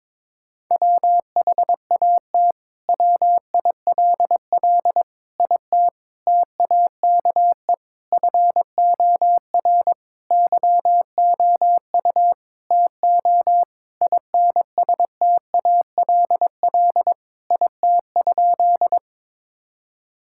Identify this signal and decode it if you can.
Morse code, heard as a keyed tone, about 22 wpm, 700 Hz. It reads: WHAT WILL IT TAKE FOR YOU TO INSTALL IT?
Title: Sentences from Top 1600 Words 22wpm